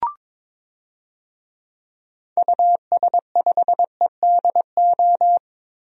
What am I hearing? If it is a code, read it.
Morse code, heard as a keyed tone, about 23 wpm, 700 Hz. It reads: US5EDO
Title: DXCC-100 Call Signs - 6 Characters - Repeated 2x - 22wpm